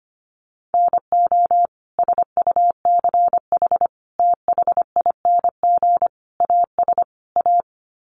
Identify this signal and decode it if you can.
Morse code, heard as a keyed tone, about 25 wpm, 700 Hz. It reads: NO SUCH THING AS A
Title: Sets of 5 Words Encore 25wpm